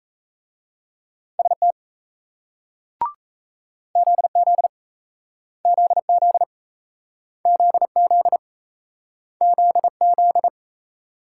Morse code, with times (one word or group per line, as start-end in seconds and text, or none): 1.4-1.7 ST
4.0-4.7 ZZ
5.7-6.4 ZZ
7.5-8.4 ZZ
9.4-10.5 ZZ